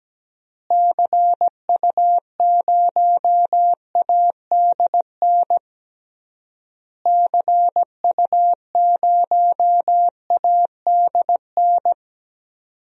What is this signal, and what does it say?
Morse code, heard as a keyed tone, about 17 wpm, 700 Hz. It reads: CU0ADN CU0ADN
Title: DXCC-100 Call Signs - 6 Characters - Repeated 3x - 17wpm